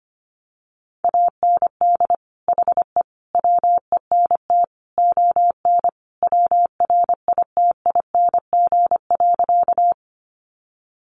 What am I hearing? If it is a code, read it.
Morse code, heard as a keyed tone, about 25 wpm, 700 Hz. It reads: AND HE WENT ON WRITING.
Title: Sentences from Top 800 Words 25wpm